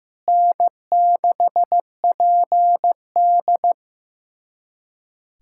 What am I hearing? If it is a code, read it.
Morse code, heard as a keyed tone, about 15 wpm, 700 Hz. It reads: N6PD
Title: USA Callsign DE Callsign - 4 Characters - 15wpm